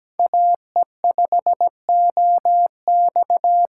Morse code, heard as a keyed tone, about 17 wpm, 700 Hz. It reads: AE5OX
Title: USA Call Signs - 5 Characters - 2x Character Spacing - 17wpm